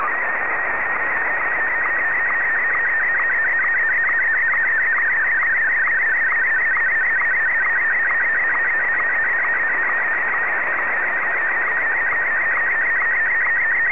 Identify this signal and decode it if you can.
SSTV (slow-scan television), Martin 1.
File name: sstv / M1-VK3DFL